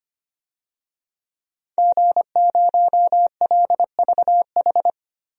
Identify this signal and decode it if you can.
Morse code, heard as a keyed tone, about 25 wpm, 700 Hz. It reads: G0LVH
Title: DXCC-100 Call Signs - 5 Characters - Repeated 2x - 25wpm